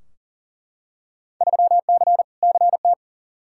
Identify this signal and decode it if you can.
Morse code, heard as a keyed tone, about 40 wpm, 700 Hz. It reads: 3C CT
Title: Field Day Exchanges Encore - 40wpm